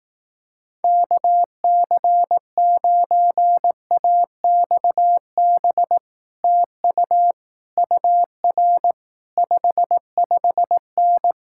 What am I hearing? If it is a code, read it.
Morse code, heard as a keyed tone, about 18 wpm, 700 Hz. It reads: KC9AXB TU UR 55N
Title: Speed-Racing Common QSO Phrases Encore - 15wpm